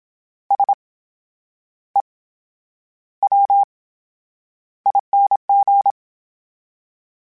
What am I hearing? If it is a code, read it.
Morse code, heard as a keyed tone, about 27 wpm, spaced as for 9 wpm, 800 Hz. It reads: SEWING
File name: Special INGs / Special INGs-mp3s / VerySlowToFaster Ing @25